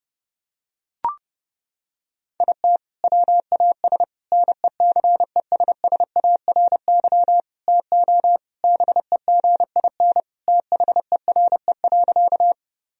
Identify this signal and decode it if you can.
Morse code, heard as a keyed tone, about 30 wpm, 700 Hz. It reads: IT WAS NECESSARY TO BEGIN THERE.
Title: Sentences from Top 1000 Words - Review Part 1 - 30wpm